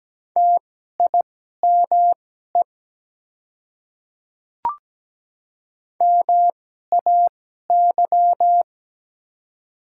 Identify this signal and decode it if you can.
Morse code, heard as a keyed tone, about 17 wpm, spaced as for 12 wpm, 700 Hz. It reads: TIME MAY
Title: 2x Character Spacing - Top 100 Words - 17wpm